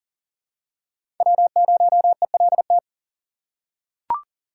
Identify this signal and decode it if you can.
Morse code, heard as a keyed tone, about 40 wpm, 700 Hz. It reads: W0ELT
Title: USA Call Signs - 5 Characters - 4x Character Spacing - 40wpm